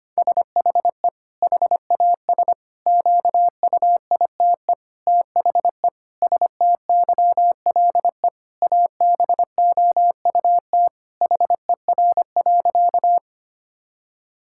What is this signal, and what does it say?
Morse code, heard as a keyed tone, about 25 wpm, 700 Hz. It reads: SHE HAS QUITE THE STYLE ABOUT HER.
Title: Sentences from Top 1000 Words - Review Part 5 - 25wpm